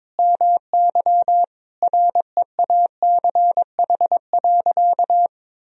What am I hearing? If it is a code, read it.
Morse code, heard as a keyed tone, about 21 wpm, 700 Hz. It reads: MY REACH.
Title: Sentences from Top 1000 Words - Review Part 2 - 22wpm